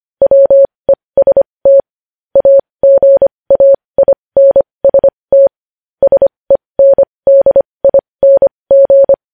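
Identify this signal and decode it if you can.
Morse code, heard as a keyed tone, about 25 wpm, spaced as for 20 wpm, 550 Hz. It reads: WEST AGAINST SENDING
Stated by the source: Morse code news headlines.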